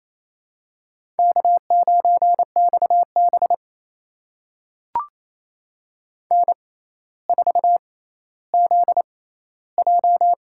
Morse code, heard as a keyed tone, about 28 wpm, spaced as for 14 wpm, 700 Hz. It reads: K9XB N4ZJ
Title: USA Call Signs - 4 Characters - 6x Character Spacing - 28wpm